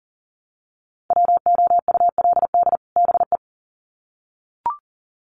Morse code, heard as a keyed tone, about 39 wpm, 700 Hz. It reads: WOULD BE
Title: Sets of 2 Words 40wpm